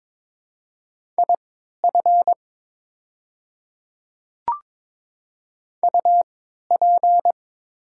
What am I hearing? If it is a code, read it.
Morse code, heard as a keyed tone, about 22 wpm, spaced as for 13 wpm, 690 Hz. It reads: IF UP
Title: 3x Character Spacing - Top 300 Words - 22wpm